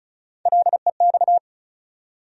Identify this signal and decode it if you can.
Morse code, heard as a keyed tone, about 36 wpm, 700 Hz. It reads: LEX
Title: USA Call Sign Postfixes - 3 Letters 35wpm